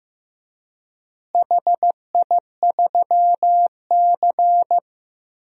Morse code, heard as a keyed tone, about 15 wpm, 700 Hz. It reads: HI3C
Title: DXCC-100 Call Signs - 4 Characters - Repeated 2x - 15wpm